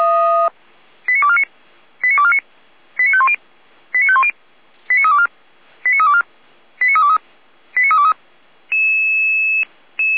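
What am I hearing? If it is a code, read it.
Five-tone selective call (ZVEI), heard as quick runs of five tones, tones of about 70 ms. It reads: 89249 89249 89510 89510 89324 89324 89323 89323
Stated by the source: ZVEI five-tone call received by SDR, recorded by feuerwehr-orlamuende.org (fire brigade Orlamünde).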